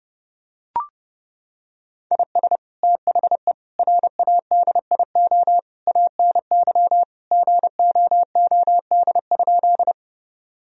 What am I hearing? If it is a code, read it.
Morse code, heard as a keyed tone, about 30 wpm, 700 Hz. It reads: IS THE RADIO ANY GOOD?
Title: Sentences from Top 1000 Words - Review Part 3 - 30wpm